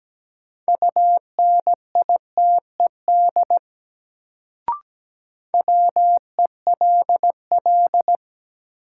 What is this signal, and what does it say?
Morse code, heard as a keyed tone, about 17 wpm, 700 Hz. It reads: UNITED WELL
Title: Sets of 2 Words - New Word Drill - 17wpm